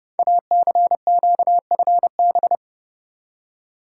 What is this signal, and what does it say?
Morse code, heard as a keyed tone, about 30 wpm, 690 Hz. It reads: ACQFB
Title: Code Groups - 5 Letters - 30wpm